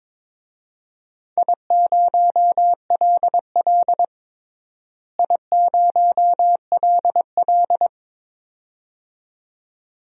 Morse code, heard as a keyed tone, about 22 wpm, 700 Hz. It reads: I0LL I0LL
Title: DXCC-100 Call Signs - 4 Characters - Repeated 2x - 22wpm